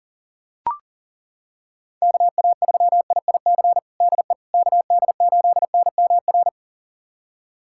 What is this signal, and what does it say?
Morse code, heard as a keyed tone, about 40 wpm, 700 Hz. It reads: KA3IIC DE KD8NMR
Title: USA Callsign DE Callsign - 6 Characters - 40wpm